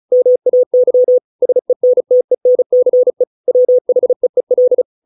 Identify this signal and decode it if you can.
Morse code, heard as a keyed tone, about 34 wpm, 500 Hz. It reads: MAY SENTENCE WHEEL